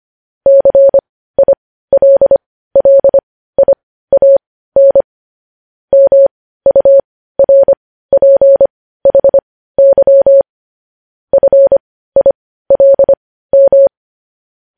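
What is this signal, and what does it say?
Morse code, heard as a keyed tone, about 25 wpm, spaced as for 15 wpm, 550 Hz. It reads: CILLIAN MURPHY FILM
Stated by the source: Morse code news headlines practice at 15.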